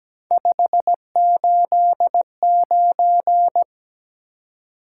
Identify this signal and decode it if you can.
Morse code, heard as a keyed tone, about 17 wpm, 700 Hz. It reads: 589
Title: Top 180 QSO Elements 17wpm